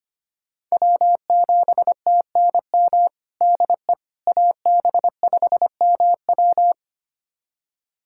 Morse code, heard as a keyed tone, about 25 wpm, 700 Hz. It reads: W7TNM DE AB5MW
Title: USA Callsign DE Callsign - 5 Characters - 25wpm